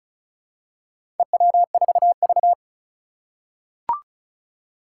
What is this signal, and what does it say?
Morse code, heard as a keyed tone, about 36 wpm, 700 Hz. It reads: EW4V
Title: Four Challenging Letter-Number Combinations 35wpm